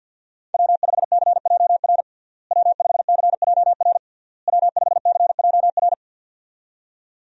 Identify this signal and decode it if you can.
Morse code, heard as a keyed tone, about 51 wpm, 700 Hz. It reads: W5CJR W5CJR W5CJR
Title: USA Call Signs - 5 Characters - Repeated 3x - 50wpm